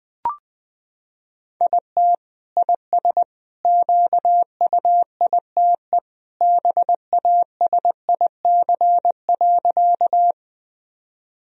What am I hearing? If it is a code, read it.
Morse code, heard as a keyed tone, about 20 wpm, 700 Hz. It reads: IT IS QUITE BASIC.
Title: Sentences from Top 1000 Words - Review Part 3 - 20wpm